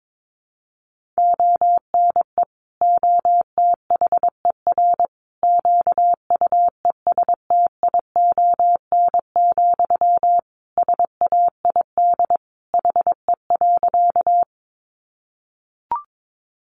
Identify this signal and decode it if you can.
Morse code, heard as a keyed tone, about 22 wpm, 700 Hz. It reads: ONE OTHER QUESTION, SAID HE.
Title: Sentences from Top 400 Words 22wpm